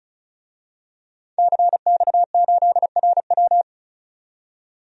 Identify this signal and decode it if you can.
Morse code, heard as a keyed tone, about 36 wpm, 700 Hz. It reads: CX8RW
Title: DXCC-100 Call Signs - 5 Characters - 35wpm